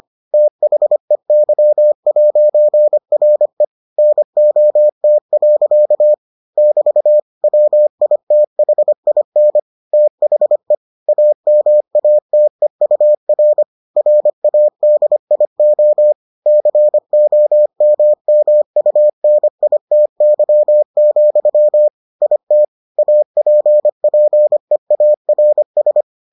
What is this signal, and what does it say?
Morse code, heard as a keyed tone, about 25 wpm, 600 Hz. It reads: THEY'RE NOT. <BT> WITHIN THE AMATEUR RADIO COMMUNITY, IT APPEARS